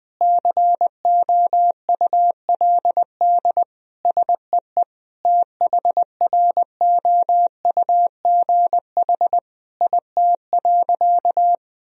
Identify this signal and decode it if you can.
Morse code, heard as a keyed tone, about 21 wpm, 700 Hz. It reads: COULD SEE THROUGH IT.
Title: Sentences from Top 1000 Words - Review Part 1 - 20wpm